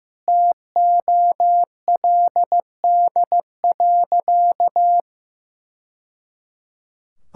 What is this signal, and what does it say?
Morse code, heard as a keyed tone, about 15 wpm, 700 Hz. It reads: TOLD.
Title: Sentences from Top 1000 Words - Review Part 6 - 15wpm